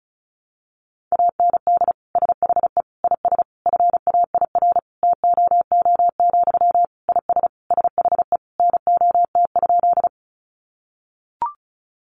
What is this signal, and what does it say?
Morse code, heard as a keyed tone, about 34 wpm, 700 Hz. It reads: AND SHE IS FAIR TOO, IS SHE NOT?
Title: Sentences from Top 1300 Words 35wpm